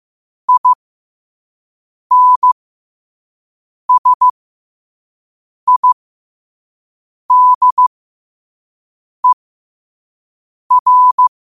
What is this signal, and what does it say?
Morse code, heard as a keyed tone, about 15 wpm, spaced as for 5 wpm, 1000 Hz. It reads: INSIDER
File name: qotdfiles / 5wpm